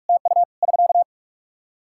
Morse code, heard as a keyed tone, about 44 wpm, 700 Hz. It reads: TU <SK>
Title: Practice QSO - Exchange 4 - Ending - 45wpm